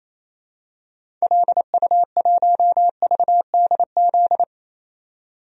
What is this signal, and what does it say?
Morse code, heard as a keyed tone, about 28 wpm, 700 Hz. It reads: LU1VDZ
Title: DXCC-100 Call Signs - 6 Characters - 28wpm